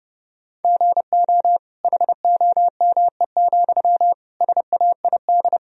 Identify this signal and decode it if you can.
Morse code, heard as a keyed tone, about 30 wpm, 700 Hz. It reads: GO HOME, SAID
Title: Sentences from Top 1000 Words - Review Part 1 - 30wpm